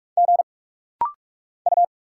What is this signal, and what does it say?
Morse code, heard as a keyed tone, about 48 wpm, 700 Hz. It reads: G U
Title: Single Letter-Number Alt - Rapid-Fire 45wpm